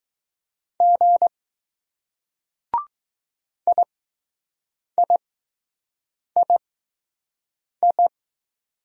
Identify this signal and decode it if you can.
Morse code, heard as a keyed tone, about 20 wpm, spaced as for 6 wpm, 700 Hz. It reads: G IIII